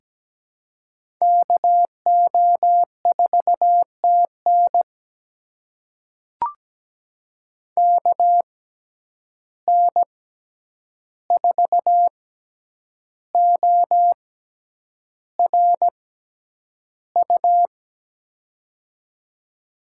Morse code, heard as a keyed tone, about 17 wpm, spaced as for 8 wpm, 700 Hz. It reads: KO4TN KN4ORU
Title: USA Call Signs - 6x Character Spacing - 17wpm